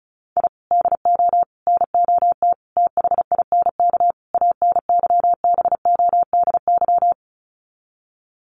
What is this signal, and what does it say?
Morse code, heard as a keyed tone, about 34 wpm, 700 Hz. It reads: I DO NOT THINK ANYBODY
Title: Sets of 5 Words 35wpm